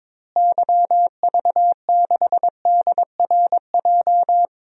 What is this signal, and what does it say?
Morse code, heard as a keyed tone, about 22 wpm, 700 Hz. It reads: YV6DRJ
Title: DXCC-100 Call Signs - 6 Characters - Repeated 2x - 22wpm